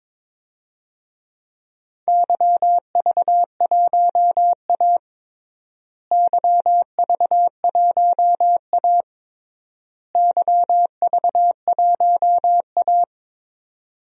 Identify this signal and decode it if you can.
Morse code, heard as a keyed tone, about 22 wpm, 700 Hz. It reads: YV1A YV1A YV1A
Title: DXCC-100 Call Signs - 4 Characters - Repeated 3x - 22wpm